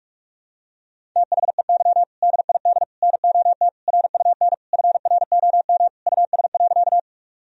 Morse code, heard as a keyed tone, about 46 wpm, 700 Hz. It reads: THEY DID NOT RUN FROM US.
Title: Sentences from Top 1000 Words - Review Part 1 - 45wpm